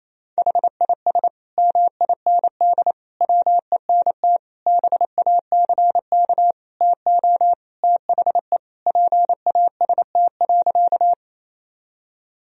Morse code, heard as a keyed tone, about 28 wpm, 700 Hz. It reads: HIS MIND WENT BACK TO THE PAST.
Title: Sentences from Top 1000 Words - Review Part 3 - 28wpm